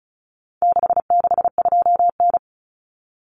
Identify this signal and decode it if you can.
Morse code, heard as a keyed tone, about 34 wpm, 700 Hz. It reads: 662N
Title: Common Cut Numbers - 4-Digit 35wpm